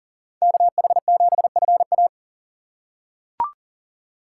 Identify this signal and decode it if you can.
Morse code, heard as a keyed tone, about 40 wpm, 700 Hz. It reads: KH7FA
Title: USA Call Signs - 5 Characters - 6x Character Spacing - 40wpm